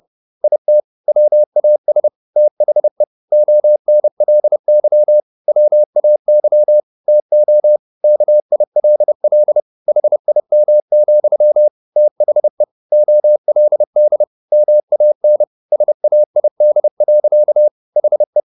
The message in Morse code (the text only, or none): IT WAS THE ONLY WAY TO KILL HIM, THE OLD MAN SAID. HE